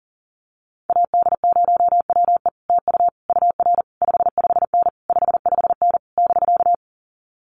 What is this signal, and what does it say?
Morse code, heard as a keyed tone, about 40 wpm, 700 Hz. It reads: AD0WE TU UR 55N 55N <BK>
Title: Common QSO Phrases Encore 40wpm